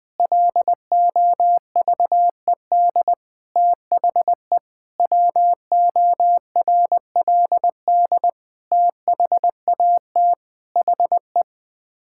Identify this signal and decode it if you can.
Morse code, heard as a keyed tone, about 20 wpm, 700 Hz. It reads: LOVED THE WORLD THAT HE
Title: Sets of 8 Words - 20wpm